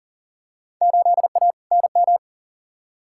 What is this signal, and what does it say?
Morse code, heard as a keyed tone, about 39 wpm, 700 Hz. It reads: 8A NM
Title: Field Day Exchanges Encore - 40wpm